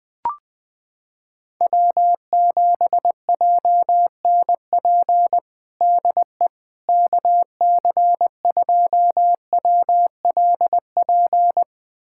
Morse code, heard as a keyed tone, about 20 wpm, 690 Hz. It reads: W7JNP DE KC2WLP
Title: USA Callsign DE Callsign 20wpm